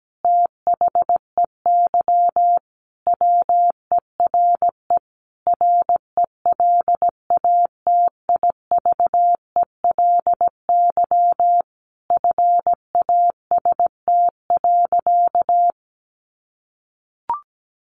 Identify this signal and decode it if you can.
Morse code, heard as a keyed tone, about 17 wpm, 700 Hz. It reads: THEY WERE RELATIVELY FAST.
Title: Sentences from Top 1100 Words 17wpm